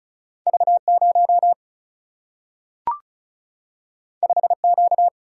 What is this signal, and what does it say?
Morse code, heard as a keyed tone, about 36 wpm, 700 Hz. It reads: V 0 5 Q
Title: Single-Letter Single-Number 35wpm